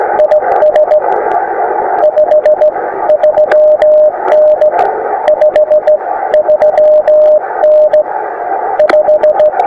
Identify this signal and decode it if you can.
Morse code, heard as a keyed tone, about 15 wpm, 610 Hz, with radio noise. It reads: IS 53N 53N 5
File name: attachments / iz0aym 20m